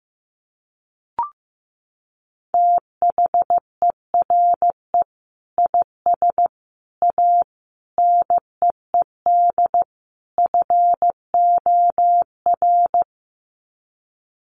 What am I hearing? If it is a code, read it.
Morse code, heard as a keyed tone, about 15 wpm, 700 Hz. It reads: THERE IS A NEED FOR
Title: Sets of 5 Words Encore2 15wpm